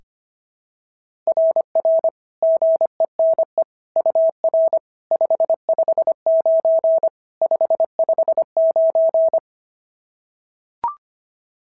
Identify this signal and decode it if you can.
Morse code, heard as a keyed tone, about 25 wpm, 650 Hz. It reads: RR GENE UR 559 559
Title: SOTA QSO Practice 25wpm